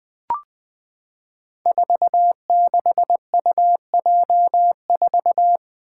Morse code, heard as a keyed tone, about 20 wpm, 700 Hz. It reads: 46UJ4
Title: Five Challenging Letter-Number Combinations 20wpm